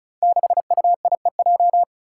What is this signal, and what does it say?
Morse code, heard as a keyed tone, about 36 wpm, 700 Hz. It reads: 6UIEJ
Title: Five Challenging Letter-Number Combinations 35wpm